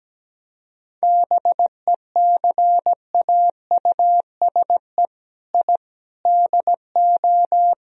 Morse code, heard as a keyed tone, about 17 wpm, 700 Hz. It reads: BECAUSE I DO
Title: Speed-Racing - Sets of 4 Words 17wpm